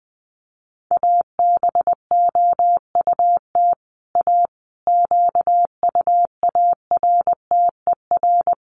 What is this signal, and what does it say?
Morse code, heard as a keyed tone, about 20 wpm, 700 Hz. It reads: ABOUT A QUARTER
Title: Sentences from Top 1300 Words 20wpm